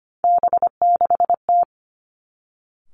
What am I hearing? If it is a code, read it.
Morse code, heard as a keyed tone, about 25 wpm, 700 Hz. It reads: B6T
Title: Three Challenging Letter-Number Combinations 25wpm